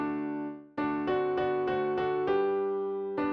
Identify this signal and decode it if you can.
There is no code.